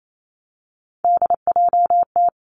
Morse code, heard as a keyed tone, about 28 wpm, 700 Hz. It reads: DJT
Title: Three Challenging Letter-Number Combinations 28wpm